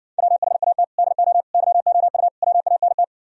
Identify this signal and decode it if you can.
Morse code, heard as a keyed tone, about 59 wpm, 700 Hz. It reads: WHAT DO YOU WANT